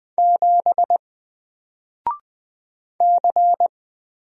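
Morse code, heard as a keyed tone, about 20 wpm, 700 Hz. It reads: 7 C
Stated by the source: Single letter and number Morse drill at 20 wpm, from N4LCD.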